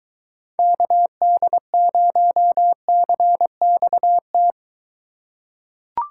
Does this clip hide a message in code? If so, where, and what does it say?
Yes: Morse code from 0.6 to 4.5 s, heard as a keyed tone, about 23 wpm, 700 Hz. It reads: KD0CXT